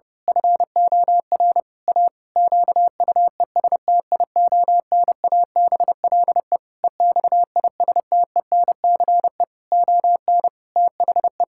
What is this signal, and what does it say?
Morse code, heard as a keyed tone, about 30 wpm, 700 Hz. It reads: FOR A QUESTIONABLE EXISTENCE ON THE
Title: Pangram Challenge v2 - 30wpm